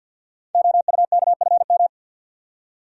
Morse code, heard as a keyed tone, about 50 wpm, 700 Hz. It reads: OUKFM
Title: Code Groups - 5 Letters - 50wpm